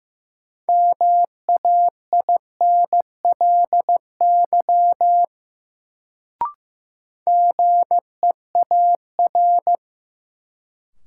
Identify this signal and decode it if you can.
Morse code, heard as a keyed tone, about 15 wpm, 700 Hz. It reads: MAINLY GEAR
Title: Sentences from Top 1400 Words - New Word Drill - 15wpm